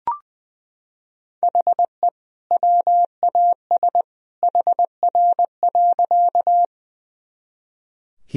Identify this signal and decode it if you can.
Morse code, heard as a keyed tone, about 20 wpm, 700 Hz. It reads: HE WAS HR.
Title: Sentences from Top 200 Words and Abbreviations 20wpm